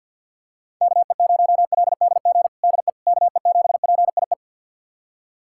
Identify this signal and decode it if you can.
Morse code, heard as a keyed tone, about 51 wpm, 700 Hz. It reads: KE0LDG DE KE7PIE
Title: USA Callsign DE Callsign - 6 Characters - 50wpm